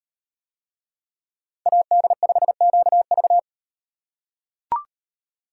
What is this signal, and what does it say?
Morse code, heard as a keyed tone, about 38 wpm, 700 Hz. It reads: AD5QV